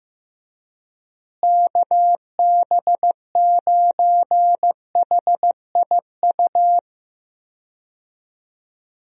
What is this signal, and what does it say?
Morse code, heard as a keyed tone, about 15 wpm, 700 Hz. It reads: KB9HIU
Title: USA Call Signs - 6 Characters - Repeated 3x - 15wpm